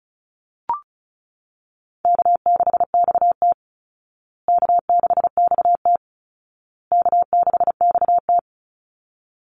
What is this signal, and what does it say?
Morse code, heard as a keyed tone, about 34 wpm, 700 Hz. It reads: K6XT K6XT K6XT
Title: Easy Call Signs Repeated 3x - 35wpm